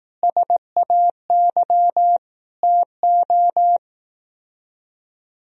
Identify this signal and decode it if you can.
Morse code, heard as a keyed tone, about 18 wpm, 700 Hz. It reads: SAY TO
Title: Speed-Racing - Sets of 5 Words 15wpm